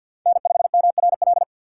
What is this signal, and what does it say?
Morse code, heard as a keyed tone, about 53 wpm, 700 Hz. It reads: N5MRL
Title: USA Call Signs - 5 Characters - 4x Character Spacing - 50wpm